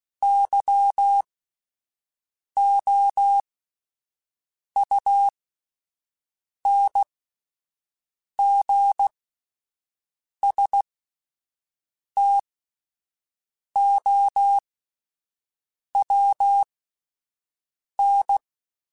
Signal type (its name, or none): Morse code